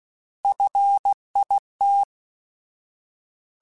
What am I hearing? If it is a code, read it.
Morse code, heard as a keyed tone, about 16 wpm, 800 Hz. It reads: FIT
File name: Alphabetical / Alphabetical mp3s / Fs @15